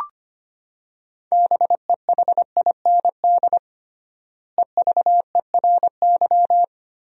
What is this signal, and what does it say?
Morse code, heard as a keyed tone, about 25 wpm, 700 Hz. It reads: BEHIND EVERY
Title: 3x Word-Spacing - Sentences from Top 500 Words - 25wpm